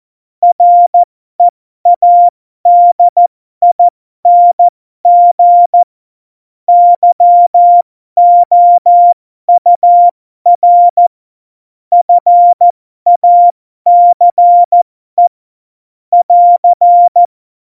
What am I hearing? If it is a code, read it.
Morse code, heard as a keyed tone, about 14 wpm, spaced as for 12 wpm, 700 Hz. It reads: READING YOUR FACE <AR>
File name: Morsekurs / Witze